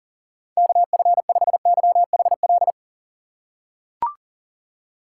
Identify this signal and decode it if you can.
Morse code, heard as a keyed tone, about 40 wpm, 700 Hz. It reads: KF5YHL
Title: USA Call Signs - 6 Characters - 3x Character Spacing - 40wpm